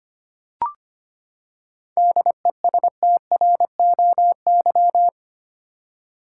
Morse code, heard as a keyed tone, about 25 wpm, 700 Hz. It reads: DESTROY
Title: Seven Letter Words - 25wpm